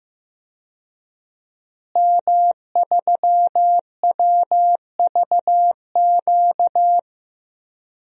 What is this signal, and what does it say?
Morse code, heard as a keyed tone, about 15 wpm, 700 Hz. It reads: M3WVQ
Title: DXCC-100 Call Signs - 5 Characters - Repeated 2x - 15wpm